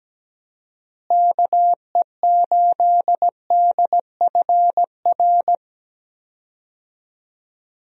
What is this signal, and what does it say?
Morse code, heard as a keyed tone, about 17 wpm, 700 Hz. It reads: KE8DFR
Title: USA Call Signs - 6 Characters - Repeated 3x - 17wpm